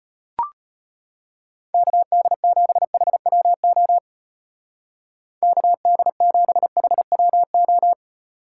Morse code, heard as a keyed tone, about 32 wpm, 700 Hz. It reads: KD7HWO KD7HWO